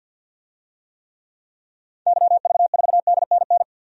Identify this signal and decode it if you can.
Morse code, heard as a keyed tone, about 51 wpm, 700 Hz. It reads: YV4DNN
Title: DXCC-100 Call Signs - 6 Characters - 50wpm